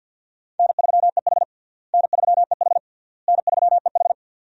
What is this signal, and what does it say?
Morse code, heard as a keyed tone, about 53 wpm, 700 Hz. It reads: N3EH N3EH N3EH
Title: USA Call Signs - 4 Characters - Repeated 3x - 50wpm